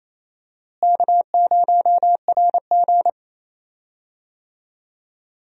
Morse code, heard as a keyed tone, about 28 wpm, 700 Hz. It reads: K0RG
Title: USA Call Signs - 4 Characters - Repeated 2x - 28wpm